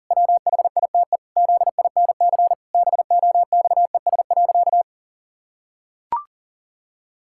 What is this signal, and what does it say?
Morse code, heard as a keyed tone, about 40 wpm, 700 Hz. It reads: WHITE ZINC BOXES.
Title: Pangram Challenge v2 - 40wpm